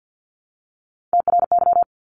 Morse code, heard as a keyed tone, about 49 wpm, 700 Hz. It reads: TRY
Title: Top 200 Words 50wpm